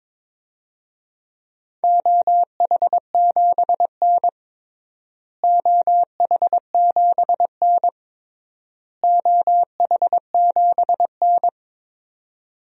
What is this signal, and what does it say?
Morse code, heard as a keyed tone, about 22 wpm, 700 Hz. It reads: OH7N OH7N OH7N
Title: DXCC-100 Call Signs - 4 Characters - Repeated 3x - 22wpm